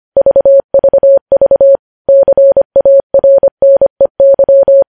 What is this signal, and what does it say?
Morse code, heard as a keyed tone, about 25 wpm, 550 Hz. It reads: VVV CARNEY
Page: news headlines in morse code at 25 wpm